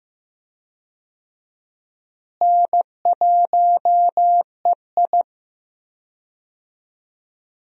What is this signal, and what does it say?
Morse code, heard as a keyed tone, about 15 wpm, 700 Hz. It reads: N1EI
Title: USA Call Signs - 4 Characters - Repeated 2x - 15wpm